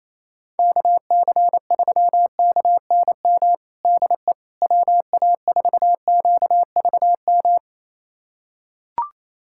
Morse code, heard as a keyed tone, about 28 wpm, 700 Hz. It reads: KC3KNM DE WA4QVM
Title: USA Callsign DE Callsign - 6 Characters - 28wpm